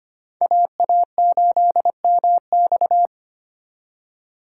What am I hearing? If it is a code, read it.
Morse code, heard as a keyed tone, about 32 wpm, 700 Hz. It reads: AA8MX